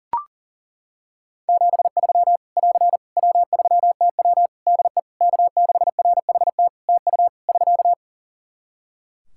Morse code, heard as a keyed tone, about 40 wpm, 700 Hz. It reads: 73 <AR> W3TW DE K6RHT TU <SK>
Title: Practice QSO - Exchange 4 - Ending - 40wpm